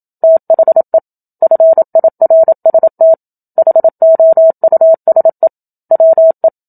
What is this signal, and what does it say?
Morse code, heard as a keyed tone, about 27 wpm, 650 Hz. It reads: THE FIRST HOUSE WE
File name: Book Two / 01-Chapter 1 - 27 WPM